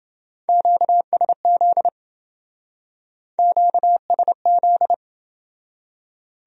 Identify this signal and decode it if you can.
Morse code, heard as a keyed tone, about 29 wpm, 700 Hz. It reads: QSZ QSZ